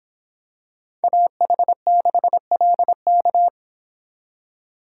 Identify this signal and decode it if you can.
Morse code, heard as a keyed tone, about 25 wpm, 700 Hz. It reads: AH6LK